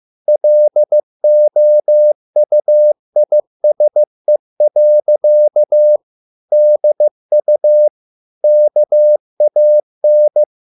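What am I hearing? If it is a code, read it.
Morse code, heard as a keyed tone, about 15 wpm, 600 Hz. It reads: LOUISE. DU KAN
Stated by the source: Morse code at a slow 15 wpm from CW med Gnister.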